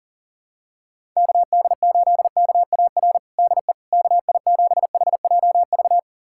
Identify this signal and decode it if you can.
Morse code, heard as a keyed tone, about 40 wpm, 700 Hz. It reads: KD8KAR DE KI7HJV
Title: USA Callsign DE Callsign - 6 Characters - 40wpm